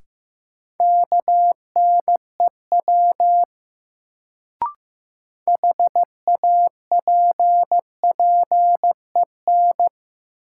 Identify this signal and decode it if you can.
Morse code, heard as a keyed tone, about 15 wpm, 700 Hz. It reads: KNEW HAPPEN
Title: Sentences from Top 400 Words - New Word Drill - 15wpm